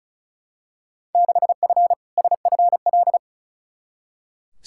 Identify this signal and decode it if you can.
Morse code, heard as a keyed tone, about 36 wpm, 700 Hz. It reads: 6F SFL
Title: Field Day Exchanges Encore - 35wpm